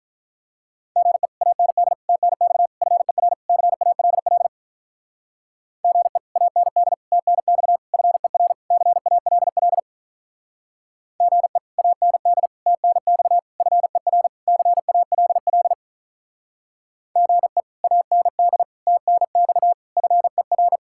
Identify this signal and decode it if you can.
Morse code, heard as a keyed tone, about 46 wpm, 700 Hz. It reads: GE AND TNX FER CALL GE AND TNX FER CALL GE AND TNX FER CALL GE AND TNX FER